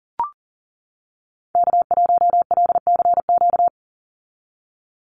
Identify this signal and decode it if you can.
Morse code, heard as a keyed tone, about 37 wpm, 690 Hz. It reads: K1LCQ
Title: USA Call Signs 40wpm